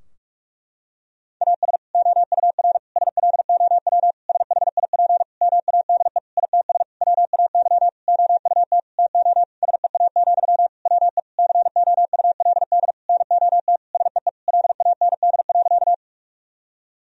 Morse code, heard as a keyed tone, about 42 wpm, 700 Hz. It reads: AS OUR SLOW SHIP MADE ITS WAY OUT TO SEA, WE COULD NOT SEE LAND.
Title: Sentences from Top 1000 Words - Review Part 2 - 45wpm